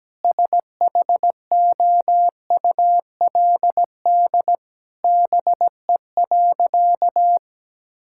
Morse code, heard as a keyed tone, about 17 wpm, 700 Hz. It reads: SHOULD BE.
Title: Sentences from Top 1000 Words - Review Part 6 - 17wpm